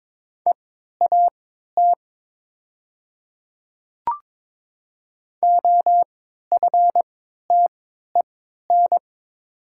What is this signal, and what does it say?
Morse code, heard as a keyed tone, about 22 wpm, spaced as for 12 wpm, 700 Hz. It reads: EAT OFTEN
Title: 3x Character Spacing - Top 300 Words - 22wpm